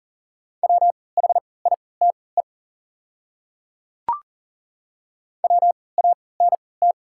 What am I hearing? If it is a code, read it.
Morse code, heard as a keyed tone, about 40 wpm, spaced as for 23 wpm, 700 Hz. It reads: WHITE WANT
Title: 3x Character Spacing - Top 300 Words - 40wpm